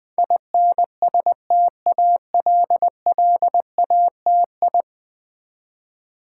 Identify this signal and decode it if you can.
Morse code, heard as a keyed tone, about 21 wpm, 700 Hz. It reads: INSTALLATI
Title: Build A Word - 5 Letters At A Time - 12-Letter Words - 20wpm